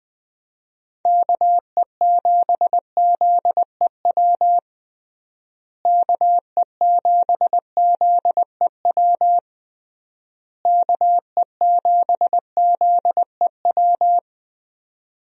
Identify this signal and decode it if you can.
Morse code, heard as a keyed tone, about 20 wpm, 700 Hz. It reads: KE7ZEW KE7ZEW KE7ZEW
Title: USA Call Signs - 6 Characters - Repeated 3x - 20wpm